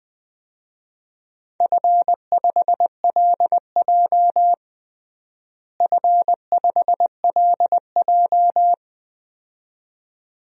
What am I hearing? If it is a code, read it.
Morse code, heard as a keyed tone, about 20 wpm, 700 Hz. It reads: F5LJ F5LJ
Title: DXCC-100 Call Signs - 4 Characters - Repeated 2x - 20wpm